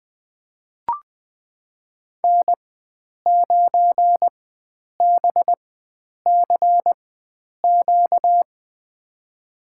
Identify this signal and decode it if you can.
Morse code, heard as a keyed tone, about 20 wpm, spaced as for 9 wpm, 700 Hz. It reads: N9BCQ
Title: USA Call Signs - 5 Characters - 4x Character Spacing - 20wpm